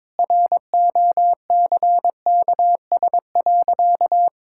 Morse code, heard as a keyed tone, about 22 wpm, 700 Hz. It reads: ROCKS.
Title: Pangram Challenge v2 - 22wpm